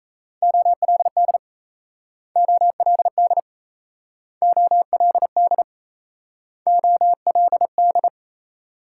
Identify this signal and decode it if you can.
Morse code, heard as a keyed tone, about 36 wpm, 700 Hz. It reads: OLD OLD OLD OLD